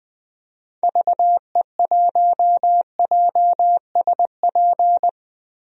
Morse code, heard as a keyed tone, about 20 wpm, 700 Hz. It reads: VE1JSP
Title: DXCC-100 Call Signs - 6 Characters - Repeated 2x - 20wpm